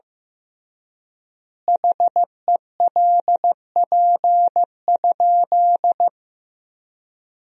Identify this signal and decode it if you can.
Morse code, heard as a keyed tone, about 16 wpm, 700 Hz. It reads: HELP?
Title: 3x Word-Spacing - Sentences from Top 200 Words - 15wpm